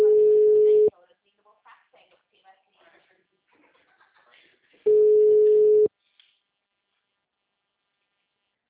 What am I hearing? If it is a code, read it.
Ringback tone (Europe, 425 Hz), with speech.
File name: out- -2020- - - .74914